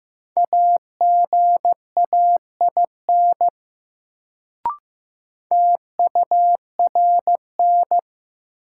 Morse code, heard as a keyed tone, about 15 wpm, 700 Hz. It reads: AGAIN TURN
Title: Sets of 4 Words - New Word Drill - 15wpm